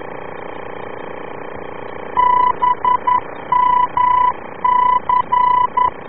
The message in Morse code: BMC